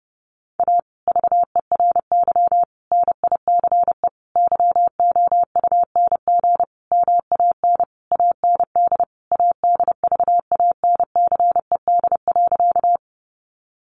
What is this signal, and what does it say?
Morse code, heard as a keyed tone, about 30 wpm, 700 Hz. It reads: A VERY NICE YOUNG MAN AND ADVANCED.
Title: Sentences from Top 1200 Words 30wpm